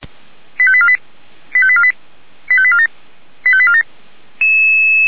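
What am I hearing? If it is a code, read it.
Five-tone selective call (ZVEI), heard as quick runs of five tones, tones of about 70 ms. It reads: 85848 85848 85847 85847
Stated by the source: ZVEI five-tone selective call received by SDR, code 89494, archived by Feuerwehr Orlamünde.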